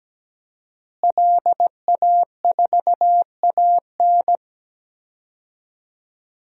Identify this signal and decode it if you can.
Morse code, heard as a keyed tone, about 17 wpm, 700 Hz. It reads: LA4AN
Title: DXCC-100 Call Signs - 5 Characters - Repeated 2x - 17wpm